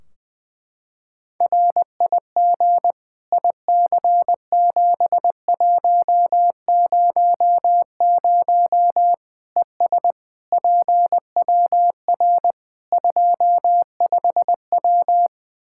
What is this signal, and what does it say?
Morse code, heard as a keyed tone, about 20 wpm, 700 Hz. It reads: RIG IC7100 ES PWR 25W
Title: K1USN SST QSO Practice 20wpm